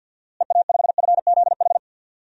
Morse code, heard as a keyed tone, about 53 wpm, 700 Hz. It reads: EA5FZH
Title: DXCC-100 Call Signs - 6 Characters - Repeated 2x - 50wpm